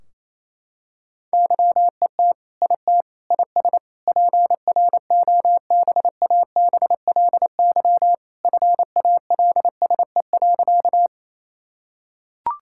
Morse code, heard as a keyed tone, about 28 wpm, 700 Hz. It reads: YET IT IS PROBABLY FALSE.
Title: Sentences from Top 1600 Words 28wpm